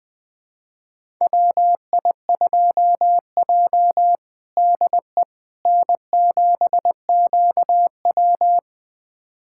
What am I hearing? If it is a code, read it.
Morse code, heard as a keyed tone, about 20 wpm, 700 Hz. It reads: WI2J DE N7QW
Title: USA Callsign DE Callsign - 4 Characters - 20wpm